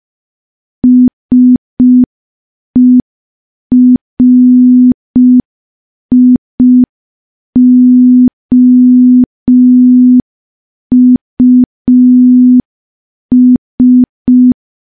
Morse code, heard as a keyed tone, about 5 wpm, 250 Hz. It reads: SERIOUS
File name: morse / morse-training-05WPM